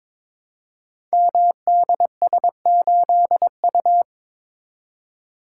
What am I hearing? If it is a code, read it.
Morse code, heard as a keyed tone, about 22 wpm, 700 Hz. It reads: MDS8U
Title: Code Groups - 5 Letter:Numbers - 22wpm